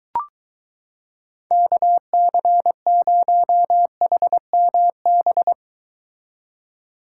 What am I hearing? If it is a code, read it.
Morse code, heard as a keyed tone, about 23 wpm, 700 Hz. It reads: KC0HMB